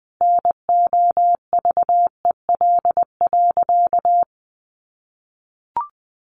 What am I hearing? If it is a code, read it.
Morse code, heard as a keyed tone, about 20 wpm, 700 Hz. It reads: NOVEL.
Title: Sentences from Top 1300 Words 20wpm